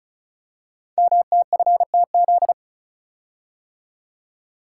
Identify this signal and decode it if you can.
Morse code, heard as a keyed tone, about 36 wpm, 700 Hz. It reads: MTFTZ
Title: Code Groups - 5 Letter:Numbers - 35wpm